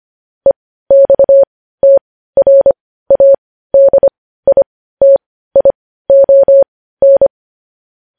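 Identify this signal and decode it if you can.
Morse code, heard as a keyed tone, about 25 wpm, spaced as for 15 wpm, 550 Hz. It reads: EXTRADITION
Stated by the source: Morse code news headlines at 15 wpm.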